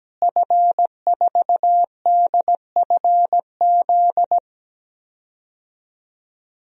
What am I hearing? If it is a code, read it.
Morse code, heard as a keyed tone, about 17 wpm, 700 Hz. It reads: F4DFZ
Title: DXCC-100 Call Signs - 5 Characters - Repeated 2x - 17wpm